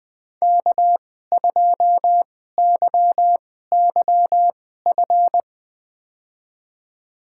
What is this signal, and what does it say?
Morse code, heard as a keyed tone, about 20 wpm, spaced as for 14 wpm, 700 Hz. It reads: K2YYF
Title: USA Call Signs - 5 Characters - 2x Character Spacing - 20wpm